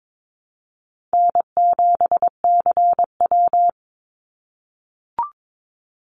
Morse code, heard as a keyed tone, about 22 wpm, 700 Hz. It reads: N7CW
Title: NA SOTA Activator Callsigns 22wpm